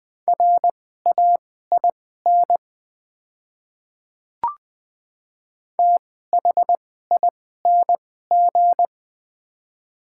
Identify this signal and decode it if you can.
Morse code, heard as a keyed tone, about 21 wpm, spaced as for 15 wpm, 700 Hz. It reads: RAIN THING
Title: 2x Character Spacing - Top 300 Words - 20wpm